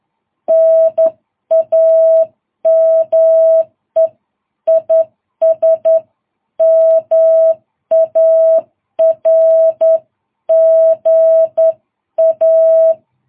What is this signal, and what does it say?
Morse code, heard as a keyed tone, about 10 wpm, 660 Hz. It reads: NAME IS MARGA
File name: jh8cbh / 2022cw